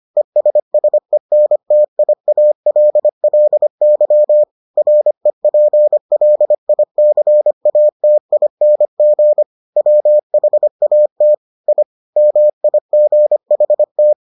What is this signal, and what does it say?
Morse code, heard as a keyed tone, about 25 wpm, 600 Hz. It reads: ESSENTIALLY REPLICATING WHAT I MIGHT